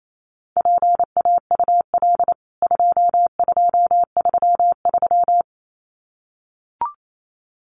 Morse code, heard as a keyed tone, about 28 wpm, 700 Hz. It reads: PAUL 2233
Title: CWT Exchange 28wpm